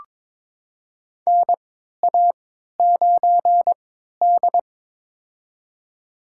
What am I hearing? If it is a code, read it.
Morse code, heard as a keyed tone, about 22 wpm, spaced as for 12 wpm, 700 Hz. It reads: NA9D
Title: USA Call Signs - 4 Characters - 3x Character Spacing - 22wpm